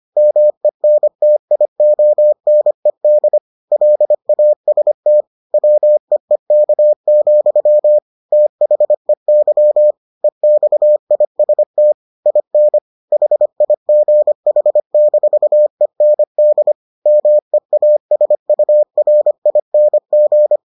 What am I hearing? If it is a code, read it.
Morse code, heard as a keyed tone, about 25 wpm, 600 Hz. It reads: MENTIONED LAST WEEK, THEY EXIST IN HIGH-END MEASURING